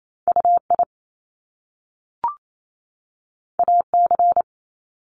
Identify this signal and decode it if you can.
Morse code, heard as a keyed tone, about 28 wpm, 700 Hz. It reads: UI AC
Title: Top Two Letter Combinations in Words 28wpm